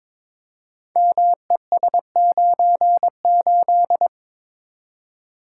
Morse code, heard as a keyed tone, about 22 wpm, 700 Hz. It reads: MES98
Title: Code Groups - 5 Letter:Numbers - 22wpm